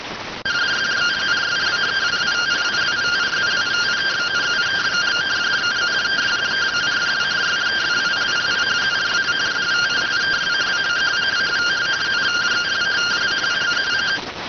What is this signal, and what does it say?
Two-tone teleprinter signal, about 170 Hz shift.